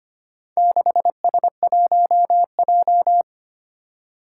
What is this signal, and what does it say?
Morse code, heard as a keyed tone, about 25 wpm, 700 Hz. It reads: 6S1J
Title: Four Challenging Letter-Number Combinations 25wpm